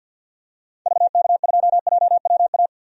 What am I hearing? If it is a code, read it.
Morse code, heard as a keyed tone, about 51 wpm, 700 Hz. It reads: VK2JWA
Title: DXCC-100 Call Signs - 6 Characters - 50wpm